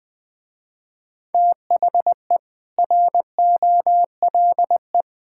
Morse code, heard as a keyed tone, about 20 wpm, 700 Hz. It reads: THE ROLE
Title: Sentences with 4-Word Phrases - Spoken Beginning - 20wpm